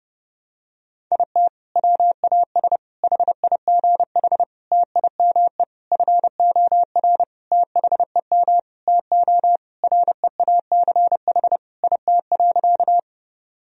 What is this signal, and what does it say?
Morse code, heard as a keyed tone, about 30 wpm, 700 Hz. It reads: IT WAS HIGH TIME FOR THEM TO REACH IT.
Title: Sentences from Top 1000 Words - Review Part 2 - 30wpm